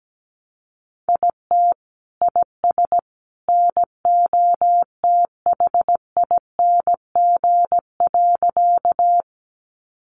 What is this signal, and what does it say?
Morse code, heard as a keyed tone, about 17 wpm, 700 Hz. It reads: IT IS NOTHING.
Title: Sentences from Top 500 Words 17wpm